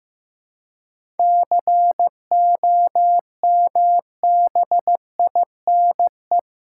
Morse code, heard as a keyed tone, about 15 wpm, 700 Hz. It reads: COMBINE
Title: Seven Letter Words - 15wpm